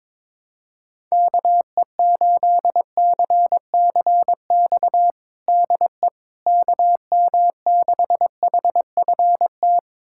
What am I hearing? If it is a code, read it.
Morse code, heard as a keyed tone, about 22 wpm, 700 Hz. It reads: KE8CCX DE KM6HFT
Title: USA Callsign DE Callsign - 6 Characters - 22wpm